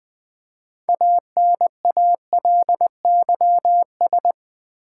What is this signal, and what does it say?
Morse code, heard as a keyed tone, about 20 wpm, 700 Hz. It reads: ANALYS
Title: Build A Word - 3 Letters At A Time - 8-Letter Words - 20wpm